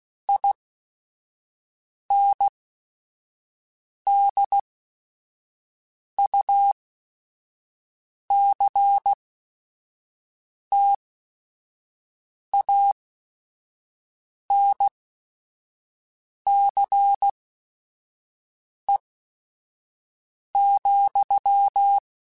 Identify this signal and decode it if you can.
Morse code, heard as a keyed tone, about 16 wpm, spaced as for 5 wpm, 800 Hz. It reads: INDUCTANCE,
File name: Technical Subjects / Technical Subjects-mp3s / XL @5